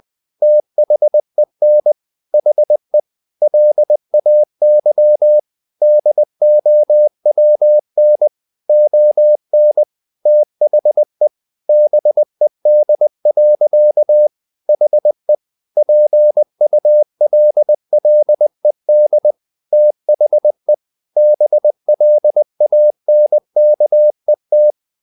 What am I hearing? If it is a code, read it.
Morse code, heard as a keyed tone, about 20 wpm, 600 Hz. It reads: THEN HE LAY DOWN ON THE BED. HE PULLED THE BLANKET
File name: the old man and the sea part 2 20wpm